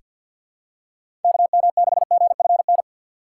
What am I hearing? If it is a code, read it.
Morse code, heard as a keyed tone, about 47 wpm, 700 Hz. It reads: KM6GFN
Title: USA Call Signs - 6 Characters - 6x Character Spacing - 50wpm